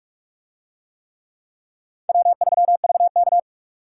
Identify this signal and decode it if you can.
Morse code, heard as a keyed tone, about 44 wpm, 700 Hz. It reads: W3VK